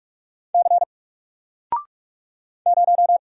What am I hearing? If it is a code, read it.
Morse code, heard as a keyed tone, about 44 wpm, 700 Hz. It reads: C 0
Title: Single Letter-Number 45wpm